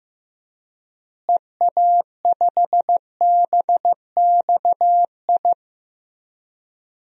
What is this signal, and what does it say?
Morse code, heard as a keyed tone, about 15 wpm, 700 Hz. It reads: EA5BXI
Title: DXCC-100 Call Signs - 6 Characters - Repeated 3x - 15wpm